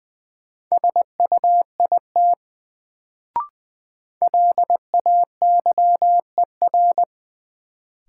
Morse code, heard as a keyed tone, about 20 wpm, 700 Hz. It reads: SUIT LAYER
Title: Sentences from Top 1600 Words - New Word Drill - 20wpm